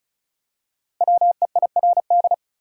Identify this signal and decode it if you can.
Morse code, heard as a keyed tone, about 36 wpm, 700 Hz. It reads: WEIRD
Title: Five Letter Words - 35wpm